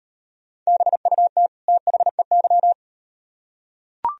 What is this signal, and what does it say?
Morse code, heard as a keyed tone, about 38 wpm, 700 Hz. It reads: BUT THEY